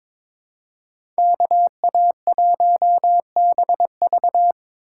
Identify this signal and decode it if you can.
Morse code, heard as a keyed tone, about 22 wpm, 700 Hz. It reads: KA1BV
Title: USA Call Signs - 5 Characters - 2x Character Spacing - 22wpm